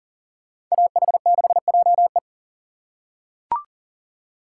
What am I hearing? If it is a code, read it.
Morse code, heard as a keyed tone, about 40 wpm, 700 Hz. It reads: AH6JE
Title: USA Call Signs - 5 Characters - 4x Character Spacing - 40wpm